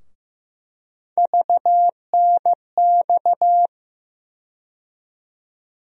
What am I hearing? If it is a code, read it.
Morse code, heard as a keyed tone, about 16 wpm, 700 Hz. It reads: VNX
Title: USA Call Sign Postfixes - 3 Letters 15wpm